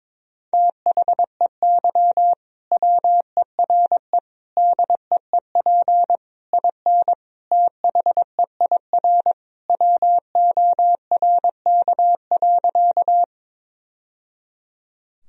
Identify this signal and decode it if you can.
Morse code, heard as a keyed tone, about 23 wpm, 700 Hz. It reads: THEY WERE DEEP IN THEIR WORK.
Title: Sentences from Top 1000 Words - Review Part 3 - 22wpm